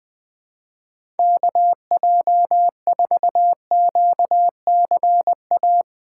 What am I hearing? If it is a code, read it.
Morse code, heard as a keyed tone, about 20 wpm, 700 Hz. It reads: KJ4QCA
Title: USA Call Signs - 6 Characters - 4x Character Spacing - 20wpm